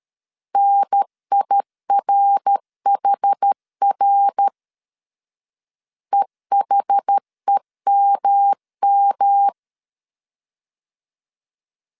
Morse code, heard as a keyed tone, about 13 wpm, 800 Hz. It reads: NIRHR EHEMM
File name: CW / DET / 1 3 1500C S B 12wpm 2828